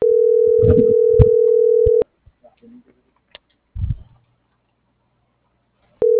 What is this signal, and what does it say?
Ringback tone (North America), 440 + 480 Hz.
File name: CENTRAL MABE / out cristopher.carrera